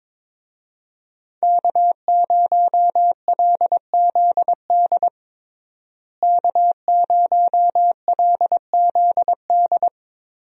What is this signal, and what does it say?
Morse code, heard as a keyed tone, about 22 wpm, 700 Hz. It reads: K0LZD K0LZD
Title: USA Call Signs - 5 Characters - Repeated 3x - 22wpm